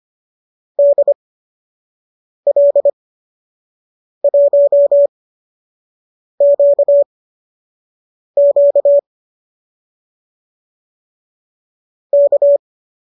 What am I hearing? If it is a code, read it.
Morse code, heard as a keyed tone, about 25 wpm, spaced as for 6 wpm, 570 Hz. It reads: DL1QQ K
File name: qso012 06